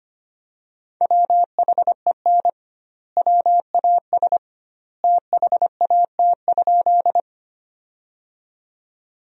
Morse code, heard as a keyed tone, about 25 wpm, 700 Hz. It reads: WHEN WAS THAT?
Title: 2x Word-Spacing - Sentences from Top 100 Words - 25wpm